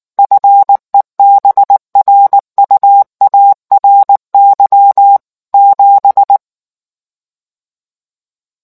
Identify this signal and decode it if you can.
Morse code, heard as a keyed tone, about 19 wpm, 800 Hz. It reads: FEBRUARY 7